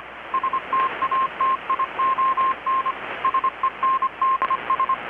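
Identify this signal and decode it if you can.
Morse code, heard as a keyed tone, about 25 wpm, 1060 Hz. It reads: STATION SENDS
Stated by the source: Morse from KPH's Night of Nights XX maritime radiotelegraphy event, received in Chiapas, Mexico about 3,800 km away on a Kenwood R-600 with a grounded 10 m long-wire antenna.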